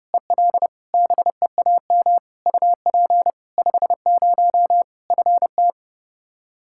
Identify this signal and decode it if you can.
Morse code, heard as a keyed tone, about 30 wpm, 700 Hz. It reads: EL BEAM UP 50 FT
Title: Practice QSO - Exchange 2 - Info - 30wpm